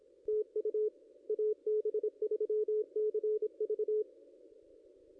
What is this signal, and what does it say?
Morse code, heard as a keyed tone, about 24 wpm, 430 Hz, with radio noise. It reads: TU AB3CV
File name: contests / ss-cw-k5tr-op-n5rz -0200